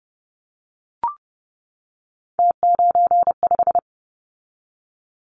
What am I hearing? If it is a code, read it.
Morse code, heard as a keyed tone, about 30 wpm, 700 Hz. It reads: T95
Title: Callsign Prefixes for Top 100 DXCC - 30wpm